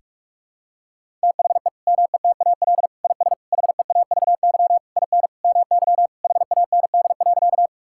Morse code, heard as a keyed tone, about 41 wpm, 700 Hz. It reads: THE METAL IS HEAVY IN MY HAND.
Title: Sentences from Top 1000 Words - Review Part 6 - 45wpm